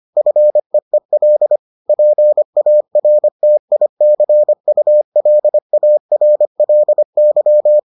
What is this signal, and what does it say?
Morse code, heard as a keyed tone, about 25 wpm, 600 Hz. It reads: FEEL PARTICULARLY